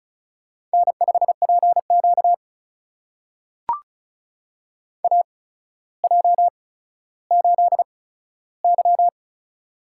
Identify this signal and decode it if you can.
Morse code, heard as a keyed tone, about 36 wpm, spaced as for 15 wpm, 700 Hz. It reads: N5PQ AJ8Y
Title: USA Call Signs - 4 Characters - 8x Character Spacing - 35wpm